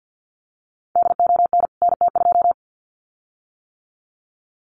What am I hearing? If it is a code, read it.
Morse code, heard as a keyed tone, about 49 wpm, 700 Hz. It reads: DON NT2
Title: CWT Exchange 50wpm